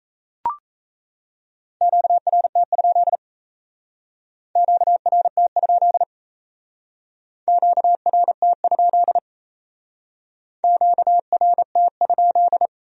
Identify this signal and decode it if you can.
Morse code, heard as a keyed tone, about 34 wpm, 700 Hz. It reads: QRT? QRT? QRT? QRT?
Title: Speed-Racing Q-Codes - 28wpm